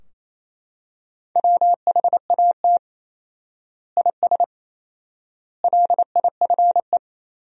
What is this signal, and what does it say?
Morse code, heard as a keyed tone, about 28 wpm, 700 Hz. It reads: WHAT IS LIFE
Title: 4x Word-Spacing - Sentences from Top 300 Words - 28wpm